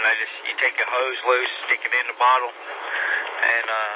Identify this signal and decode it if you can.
Emergency Alert System (SAME) tones, heard as harsh buzzing bursts, with speech. It is heard from 0.4 to 3.2 s.